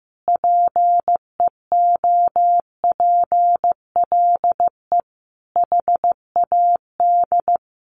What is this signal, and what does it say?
Morse code, heard as a keyed tone, about 15 wpm, 700 Hz. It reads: PEOPLE HAD
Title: Sentences from Top 1400 Words 15wpm